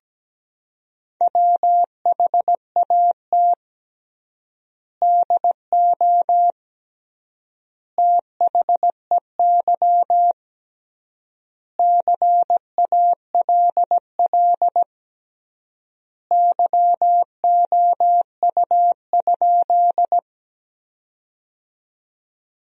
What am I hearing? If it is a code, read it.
Morse code, heard as a keyed tone, about 17 wpm, 700 Hz. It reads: WHAT DO THEY CALL YOU?
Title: 3x Word-Spacing - Sentences from Top 100 Words - 17wpm